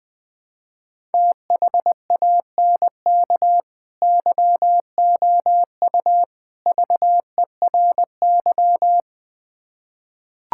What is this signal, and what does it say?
Morse code, heard as a keyed tone, about 21 wpm, 700 Hz. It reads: THANK YOU VERY
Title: Build A Phrase - 3-Word Phrases - 20wpm